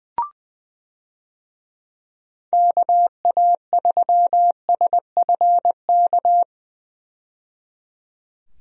Morse code, heard as a keyed tone, about 20 wpm, 700 Hz. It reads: KA3SFK
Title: USA Call Signs - 6 Characters - 20wpm